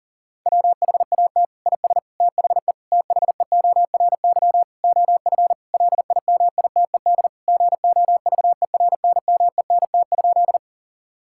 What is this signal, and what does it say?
Morse code, heard as a keyed tone, about 40 wpm, 690 Hz. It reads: WHAT IS THE THEORY OF LIMITED GOVERNMENT?
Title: Sentences from Top 1000 Words - Review Part 6 - 40wpm